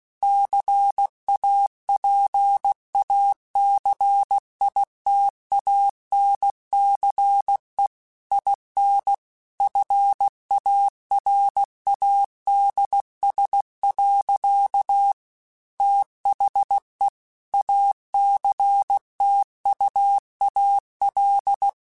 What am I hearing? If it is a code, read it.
Morse code, heard as a keyed tone, about 16 wpm, 800 Hz. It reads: CAPACITANCE IN FARADS. THE ACTUAL
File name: Technical Subjects / Technical Subjects-mp3s / XC @15